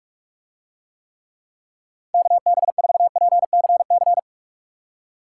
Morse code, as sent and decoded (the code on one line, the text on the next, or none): -.- -... ....- .--. -.-. -.-.
KB4PCC